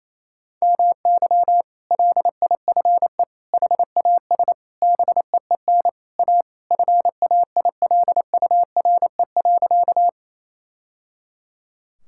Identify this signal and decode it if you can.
Morse code, heard as a keyed tone, about 28 wpm, 700 Hz. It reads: MY LIFE HAS BEEN A FAILURE.
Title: Sentences from Top 1000 Words - Review Part 6 - 28wpm